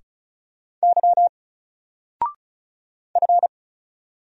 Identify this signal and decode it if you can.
Morse code, heard as a keyed tone, about 34 wpm, 700 Hz. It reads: Y F
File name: Single-Letter-Number-35wpm